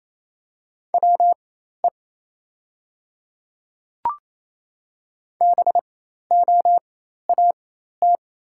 Morse code, heard as a keyed tone, about 28 wpm, spaced as for 13 wpm, 700 Hz. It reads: WE BOAT
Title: 4x Character Spacing - Top 500 Words - 28wpm